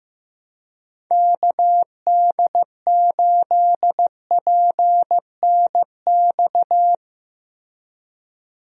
Morse code, heard as a keyed tone, about 15 wpm, 700 Hz. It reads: KD8PNX
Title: USA Call Signs - 6 Characters - Repeated 3x - 15wpm